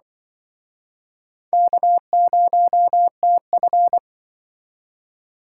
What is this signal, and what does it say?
Morse code, heard as a keyed tone, about 25 wpm, 700 Hz. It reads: K0TF
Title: Speed Racing - USA 1x2 Call Signs 20wpm